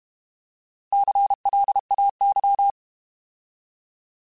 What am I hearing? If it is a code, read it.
Morse code, heard as a keyed tone, about 32 wpm, 800 Hz. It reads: CLAY
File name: Alphabetical / Alphabetical mp3s / Cs @30